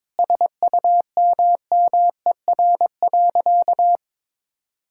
Morse code, heard as a keyed tone, about 22 wpm, 700 Hz. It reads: SUMMER.
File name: Sentences with 9-Word Phrases - 22wpm